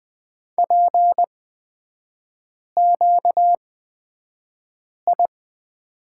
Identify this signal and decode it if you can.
Morse code, heard as a keyed tone, about 20 wpm, spaced as for 5 wpm, 700 Hz. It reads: PQI
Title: 250 Lesson 40e - BK - Three Characters 20wpm